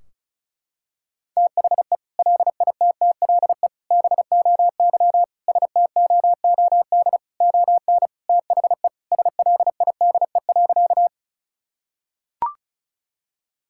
Morse code, heard as a keyed tone, about 34 wpm, 700 Hz. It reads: THE LITTLE BOY STOOD ON THE SLIDE.
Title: Sentences from Top 1600 Words 35wpm